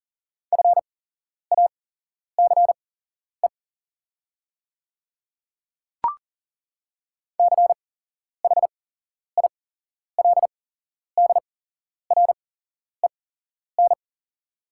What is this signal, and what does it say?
Morse code, heard as a keyed tone, about 41 wpm, spaced as for 11 wpm, 700 Hz. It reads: FACE CHILDREN
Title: 8x Character Spacing - Top 300 Words - 40wpm